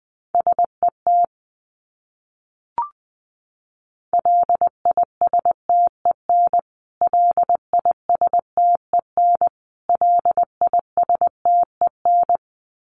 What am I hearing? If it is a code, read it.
Morse code, heard as a keyed tone, about 20 wpm, 700 Hz. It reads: SET LISTEN LISTEN LISTEN
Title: Top 500 Words Repeated 20wpm